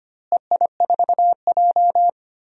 Morse code, heard as a keyed tone, about 25 wpm, 700 Hz. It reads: EI4J
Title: DXCC-100 Call Signs - 4 Characters - Repeated 3x - 25wpm